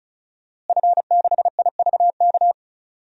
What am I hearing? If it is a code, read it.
Morse code, heard as a keyed tone, about 36 wpm, 700 Hz. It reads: F6IVK
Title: DXCC-100 Call Signs - 5 Characters - Repeated 3x - 35wpm